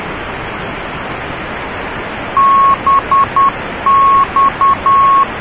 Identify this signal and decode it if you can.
Morse code, heard as a keyed tone, about 10 wpm, 1100 Hz, with noise. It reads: BX